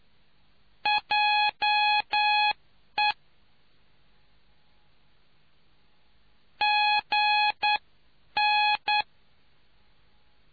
Morse code, heard as a keyed tone, about 9 wpm, spaced as for 8 wpm, 840 Hz. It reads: JE GN